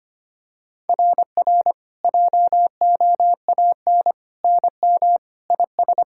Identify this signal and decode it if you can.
Morse code, heard as a keyed tone, about 25 wpm, 690 Hz. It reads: RR JOAN NM IS
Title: SOTA QSO Practice 25wpm